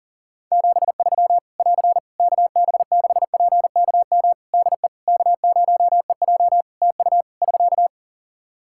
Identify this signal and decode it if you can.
Morse code, heard as a keyed tone, about 40 wpm, 700 Hz. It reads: 73 <AR> KB6PKM DE K0EJ TU <SK>
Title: Practice QSO - Exchange 4 - Ending - 40wpm